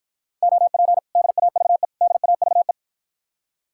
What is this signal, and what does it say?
Morse code, heard as a keyed tone, about 52 wpm, 700 Hz. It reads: OP DAVE DAVE